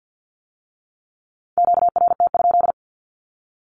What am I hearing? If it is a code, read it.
Morse code, heard as a keyed tone, about 50 wpm, 700 Hz. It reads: QRT?
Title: Q-Codes with Spoken Meaning - 50wpm